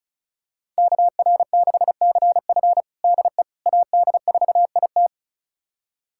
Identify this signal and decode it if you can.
Morse code, heard as a keyed tone, about 36 wpm, 690 Hz. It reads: KR6CF DE AD4IT
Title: USA Callsign DE Callsign - 5 Characters - 35wpm